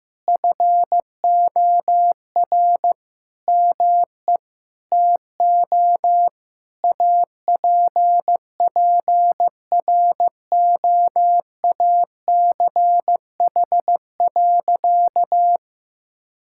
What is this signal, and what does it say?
Morse code, heard as a keyed tone, about 16 wpm, 700 Hz. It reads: FOR ME TO APPROACH.
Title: Sentences from Top 1000 Words - Review Part 6 - 15wpm